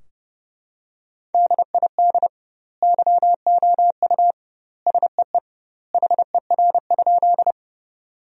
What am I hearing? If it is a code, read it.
Morse code, heard as a keyed tone, about 31 wpm, 700 Hz. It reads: DID YOU SEE HER?
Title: 2x Word-Spacing - Sentences from Top 100 Words - 30wpm